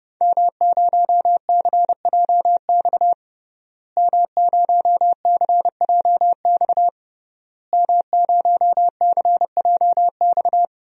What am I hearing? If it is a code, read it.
Morse code, heard as a keyed tone, about 30 wpm, 700 Hz. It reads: M0CJX M0CJX M0CJX
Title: DXCC-100 Call Signs - 5 Characters - Repeated 3x - 30wpm